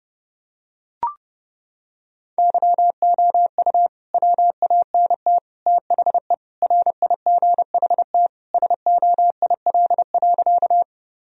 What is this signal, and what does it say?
Morse code, heard as a keyed tone, about 32 wpm, 700 Hz. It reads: YOU WANT THE RIGHT SOIL.
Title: Sentences from Top 1000 Words - Review Part 4 - 30wpm